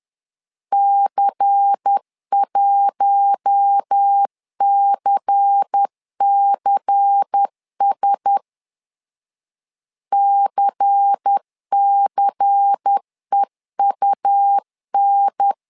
Morse code, heard as a keyed tone, about 10 wpm, 800 Hz. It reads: C1CCS CCEUN
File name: LEZIONI / 5 LBVC / 5 LBVC 10wpm 2349